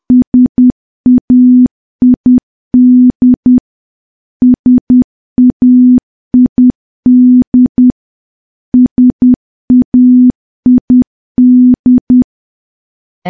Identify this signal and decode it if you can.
Morse code, heard as a keyed tone, about 10 wpm, 250 Hz. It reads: SAID SAID SAID
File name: morse / morse-training-10WPM